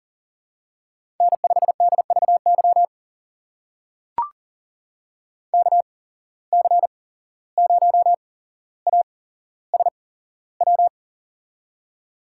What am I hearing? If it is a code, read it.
Morse code, heard as a keyed tone, about 40 wpm, spaced as for 16 wpm, 700 Hz. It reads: N5DVY KC0ASW
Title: USA Call Signs - 8x Character Spacing - 40wpm